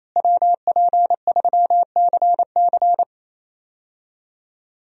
Morse code, heard as a keyed tone, about 28 wpm, 700 Hz. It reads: WP3CC
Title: USA Call Signs - 5 Characters - Repeated 2x - 28wpm